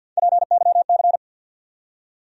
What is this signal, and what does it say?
Morse code, heard as a keyed tone, about 51 wpm, 700 Hz. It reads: PYC
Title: USA Call Sign Postfixes - 3 Letters 50wpm